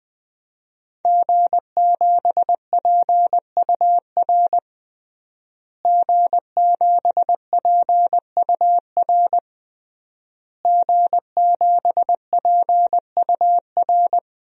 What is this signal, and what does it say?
Morse code, heard as a keyed tone, about 20 wpm, 700 Hz. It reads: G7PUR G7PUR G7PUR
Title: DXCC-100 Call Signs - 5 Characters - Repeated 3x - 20wpm